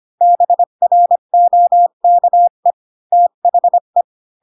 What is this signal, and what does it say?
Morse code, heard as a keyed tone, about 25 wpm, 700 Hz. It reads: BROKE THE
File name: Pinocchio / Pinocchio ch5 20 25 ik1hge